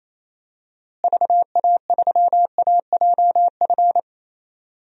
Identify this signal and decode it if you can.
Morse code, heard as a keyed tone, about 27 wpm, 700 Hz. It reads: VA3AJF
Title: DXCC-100 Call Signs - 6 Characters - Repeated 3x - 28wpm